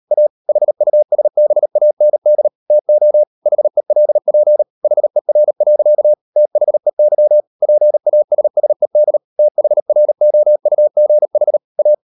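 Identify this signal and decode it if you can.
Morse code, heard as a keyed tone, about 38 wpm, 600 Hz. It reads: A HUSBAND TO HELP HER. THEY PASSED THROUGH A
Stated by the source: Morse code practice at 38 WPM, sending Ken Follett's novel Never in English, from CW med Gnister.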